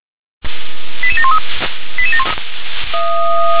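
Five-tone selective call (ZVEI), heard as quick runs of five tones, tones of about 80 ms. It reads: 88713
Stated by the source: ZVEI five-tone selective call 88747 received with an SDR.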